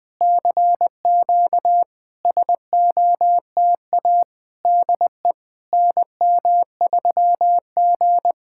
Morse code, heard as a keyed tone, about 20 wpm, 700 Hz. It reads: CQ SOTA DE NM3G
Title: SOTA QSO Practice 20wpm